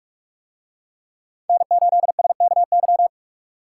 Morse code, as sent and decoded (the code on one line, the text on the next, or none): -. ---.. ... -.- -.--
N8SKY